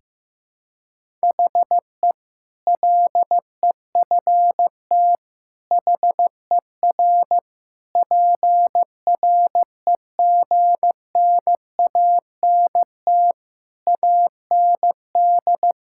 Morse code, heard as a keyed tone, about 15 wpm, 700 Hz. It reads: HE LEFT HER PREGNANT AND
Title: Sentences from Top 1000 Words - Review Part 6 - 15wpm